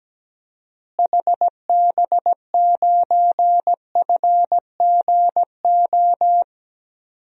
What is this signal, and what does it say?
Morse code, heard as a keyed tone, about 17 wpm, 700 Hz. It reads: HB9FGO
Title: DXCC-100 Call Signs - 6 Characters - Repeated 2x - 17wpm